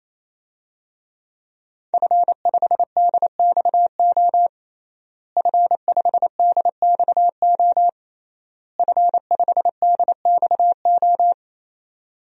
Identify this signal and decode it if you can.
Morse code, heard as a keyed tone, about 28 wpm, 700 Hz. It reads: F5DXO F5DXO F5DXO
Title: DXCC-100 Call Signs - 5 Characters - Repeated 3x - 28wpm